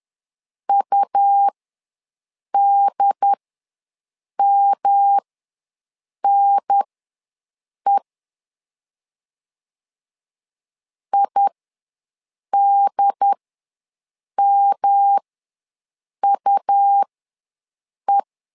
Morse code, heard as a keyed tone, about 10 wpm, spaced as for 6 wpm, 800 Hz. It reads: UDMNE IDMUE
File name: LEZIONI / 3 MWRO / 3 M ws 10wpm 2121